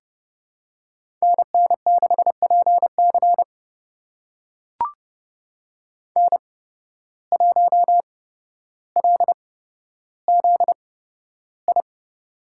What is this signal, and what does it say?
Morse code, heard as a keyed tone, about 29 wpm, spaced as for 13 wpm, 700 Hz. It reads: NN6PC N1LZI
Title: USA Call Signs - 8x Character Spacing - 30wpm